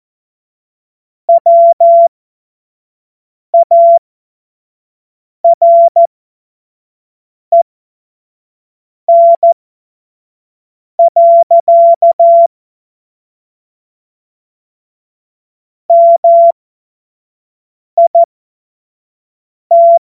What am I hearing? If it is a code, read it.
Morse code, heard as a keyed tone, about 14 wpm, spaced as for 5 wpm, 680 Hz. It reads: WAREN. MIT
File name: Morsekurs / Witze / Witz 19 März 2024 25 F70